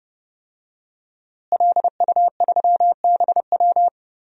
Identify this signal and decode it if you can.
Morse code, heard as a keyed tone, about 30 wpm, 700 Hz. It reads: LU3BW
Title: DXCC-100 Call Signs - 5 Characters - Repeated 3x - 30wpm